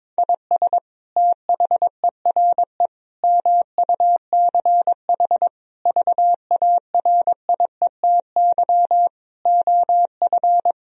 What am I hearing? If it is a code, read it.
Morse code, heard as a keyed tone, about 22 wpm, 700 Hz. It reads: IS THERE MUCH VARIETY OF
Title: Sentences from Top 1000 Words - Review Part 4 - 22wpm